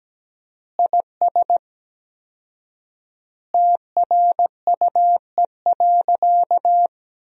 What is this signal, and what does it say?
Morse code, heard as a keyed tone, about 17 wpm, 700 Hz. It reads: IS TRUE.
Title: 4x Word-Spacing - Sentences from Top 500 Words - 17wpm